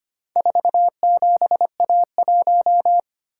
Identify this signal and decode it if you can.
Morse code, heard as a keyed tone, about 25 wpm, 700 Hz. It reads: 47A1
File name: practice / 25wpm / Four Letter-Number Combinations - 25wpm